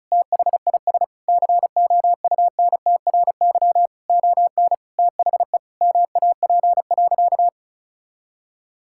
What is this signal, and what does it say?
Morse code, heard as a keyed tone, about 34 wpm, 700 Hz. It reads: THIS COUNTRY ON THE MAP.
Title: Sentences from Top 1000 Words - Review Part 6 - 35wpm